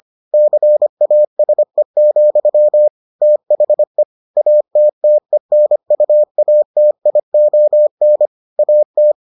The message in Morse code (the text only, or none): CASE, THE ATTENUATION AT